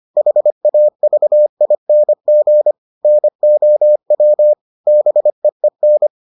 Morse code, heard as a keyed tone, about 25 wpm, 600 Hz. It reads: HAVING NOW BEEN